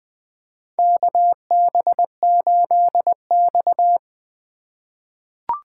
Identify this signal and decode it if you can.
Morse code, heard as a keyed tone, about 20 wpm, 700 Hz. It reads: KB8X
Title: USA Call Signs - 4x Character Spacing - 20wpm